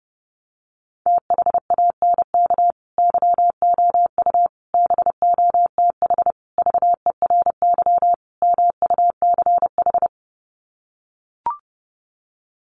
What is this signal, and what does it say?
Morse code, heard as a keyed tone, about 30 wpm, 700 Hz. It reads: THANK YOU BOTH VERY MUCH
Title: Sets of 5 Words 30wpm